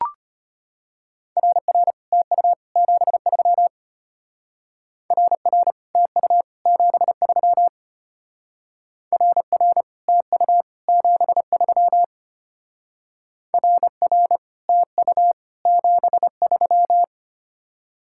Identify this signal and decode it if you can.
Morse code, heard as a keyed tone, about 32 wpm, 700 Hz. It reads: RR TU 73 RR TU 73 RR TU 73 RR TU 73